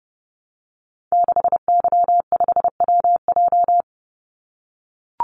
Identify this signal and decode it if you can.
Morse code, heard as a keyed tone, about 30 wpm, 700 Hz. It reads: 6Y5WJ
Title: Call Signs - 30wpm